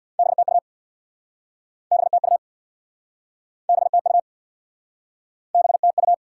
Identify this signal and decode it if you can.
Morse code, heard as a keyed tone, about 69 wpm, 700 Hz. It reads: BTU BTU BTU BTU